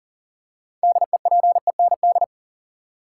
Morse code, heard as a keyed tone, about 40 wpm, 700 Hz. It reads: DEPEND
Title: Six Letter Words - 40wpm